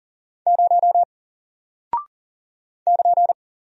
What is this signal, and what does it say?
Morse code, heard as a keyed tone, about 40 wpm, 700 Hz. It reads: 0 <KN>
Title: Single Character (Letter:Number:Punctuation:Prosign) Alt 40wpm